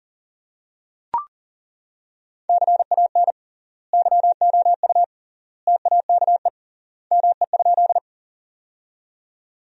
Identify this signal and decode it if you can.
Morse code, heard as a keyed tone, about 40 wpm, 700 Hz. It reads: CAN YOU TAKE ME?
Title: 3x Word-Spacing - Sentences from Top 200 Words - 40wpm